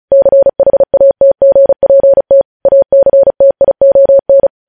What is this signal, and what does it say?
Morse code, heard as a keyed tone, about 36 wpm, 550 Hz. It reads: CHATGPT ACTION
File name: podcast / News headlines-1x-35x35